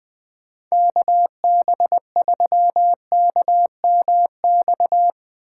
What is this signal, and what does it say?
Morse code, heard as a keyed tone, about 22 wpm, 700 Hz. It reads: KB3KMX
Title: Speed Racing - USA 2x3 Call Signs 20wpm